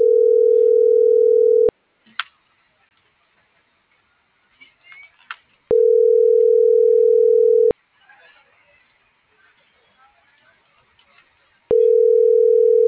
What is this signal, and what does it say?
Ringback tone (North America), 440 + 480 Hz.